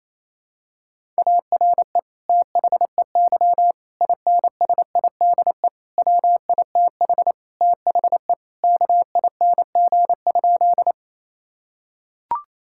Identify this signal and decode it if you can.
Morse code, heard as a keyed tone, about 28 wpm, 700 Hz. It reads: ARE THEY INSIDE WITH THE KING?
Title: Sentences from Top 1000 Words - Review Part 6 - 28wpm